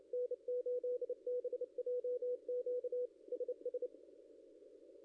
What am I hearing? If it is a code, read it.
Morse code, heard as a keyed tone, about 31 wpm, 490 Hz, with radio noise. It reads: N8BJQ SS
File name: contests / ss-cw-k5tr-op-n5rz -0700